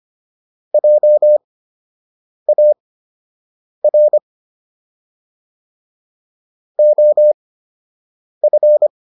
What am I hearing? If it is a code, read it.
Morse code, heard as a keyed tone, about 25 wpm, spaced as for 7 wpm, 600 Hz. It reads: JAR OF